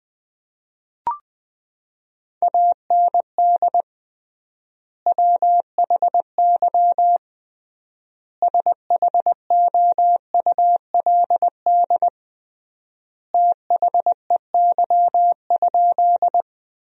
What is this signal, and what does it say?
Morse code, heard as a keyed tone, about 20 wpm, 700 Hz. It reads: AND WHY SHOULD THEY?
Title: 3x Word-Spacing - Sentences from Top 300 Words - 20wpm